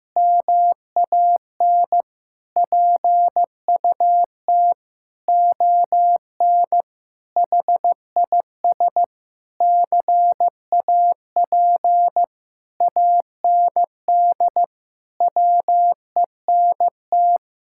Morse code, heard as a keyed tone, about 15 wpm, 700 Hz. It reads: MAN PUT ON HIS CAP AND WENT
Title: Sentences from Top 1600 Words 15wpm